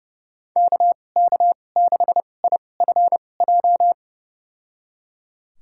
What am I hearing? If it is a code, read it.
Morse code, heard as a keyed tone, about 30 wpm, spaced as for 22 wpm, 700 Hz. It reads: KK6IFJ
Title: USA Call Signs - 6 Characters - 2x Character Spacing - 30wpm